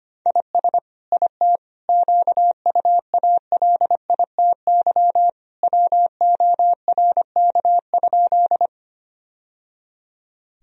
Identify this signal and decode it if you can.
Morse code, heard as a keyed tone, about 25 wpm, 700 Hz. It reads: IS IT QUALITY WORK?
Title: Sentences from Top 1000 Words - Review Part 4 - 25wpm